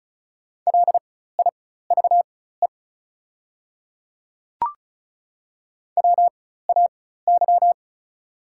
Morse code, heard as a keyed tone, about 36 wpm, spaced as for 16 wpm, 690 Hz. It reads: LIVE WAY
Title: 4x Character Spacing - Top 300 Words - 35wpm